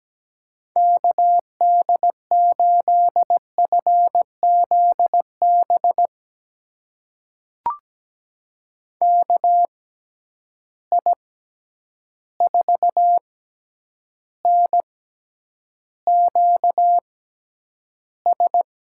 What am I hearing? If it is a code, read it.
Morse code, heard as a keyed tone, about 17 wpm, spaced as for 9 wpm, 700 Hz. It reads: KD8FZB KI4NQS
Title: USA Call Signs - 6 Characters - 6x Character Spacing - 17wpm